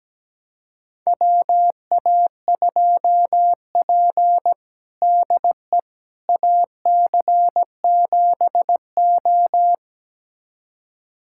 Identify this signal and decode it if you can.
Morse code, heard as a keyed tone, about 17 wpm, 700 Hz. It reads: WA2P DE AC7O
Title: USA Callsign DE Callsign - 4 Characters - 17wpm